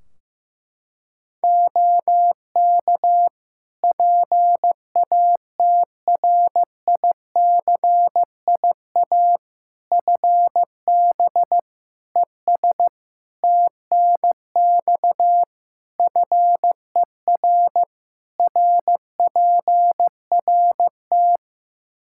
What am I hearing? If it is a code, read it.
Morse code, heard as a keyed tone, about 15 wpm, 700 Hz. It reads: OK PATRICIA FB ES TNX FER RPRT
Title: Practice QSO - Exchange 2 - Info - 15wpm